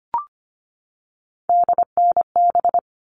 Morse code, heard as a keyed tone, about 25 wpm, 700 Hz. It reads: DNB
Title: All Cut Numbers - 3-Digit 25wpm